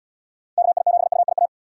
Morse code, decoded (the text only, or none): KI7UIA